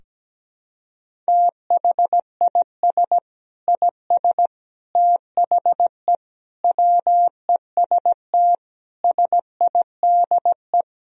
Morse code, heard as a keyed tone, about 17 wpm, 700 Hz. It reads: THIS IS THE WEST SIDE
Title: Sentences from Top 1000 Words - Review Part 2 - 17wpm